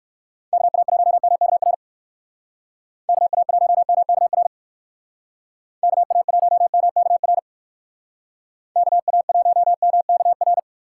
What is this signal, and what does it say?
Morse code, heard as a keyed tone, about 57 wpm, 700 Hz. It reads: KA1MKR KA1MKR KA1MKR KA1MKR